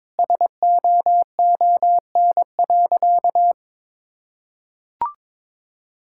Morse code, heard as a keyed tone, about 22 wpm, 700 Hz. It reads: SOON.
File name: practice / 22wpm / Sentences with 4-Word Phrases - 22wpm